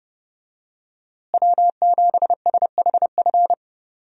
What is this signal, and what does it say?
Morse code, heard as a keyed tone, about 32 wpm, 700 Hz. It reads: W7SHF
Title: USA Call Signs - 5 Characters - Repeated 2x - 30wpm